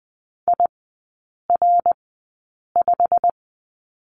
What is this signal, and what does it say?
Morse code, heard as a keyed tone, about 19 wpm, spaced as for 8 wpm, 700 Hz. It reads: IR5
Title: 100 Lesson 17e - Letter u - Three Characters 20wpm